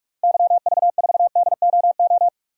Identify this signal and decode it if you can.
Morse code, heard as a keyed tone, about 46 wpm, 700 Hz. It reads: YV4DOO
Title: DXCC-100 Call Signs - 6 Characters - Repeated 3x - 45wpm